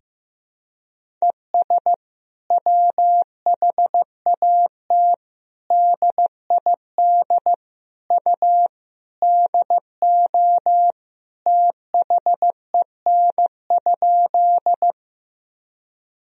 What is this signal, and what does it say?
Morse code, heard as a keyed tone, about 15 wpm, 700 Hz. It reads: ES WHAT DID U DO THEN?
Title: Sentences from Top 100 Words and Abbreviations 15wpm